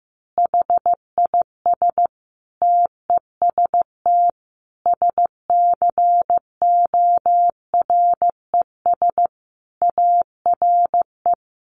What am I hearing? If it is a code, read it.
Morse code, heard as a keyed tone, about 15 wpm, 700 Hz. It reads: HIS TEST SCORES ARE
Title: Additional Easy Phrases 15wpm